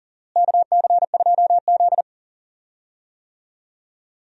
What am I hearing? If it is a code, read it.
Morse code, heard as a keyed tone, about 40 wpm, 700 Hz. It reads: KC2Z
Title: USA Call Signs - 4 Characters - Repeated 2x - 40wpm